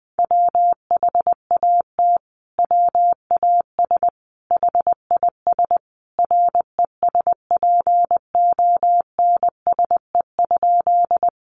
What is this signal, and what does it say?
Morse code, heard as a keyed tone, about 20 wpm, 700 Hz. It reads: WHAT WAS HIS RESPONSE?
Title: Sentences from Top 1400 Words 20wpm